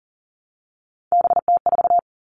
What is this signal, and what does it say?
Morse code, heard as a keyed tone, about 40 wpm, 690 Hz. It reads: BT4
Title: Three Challenging Letter-Number Combinations 40wpm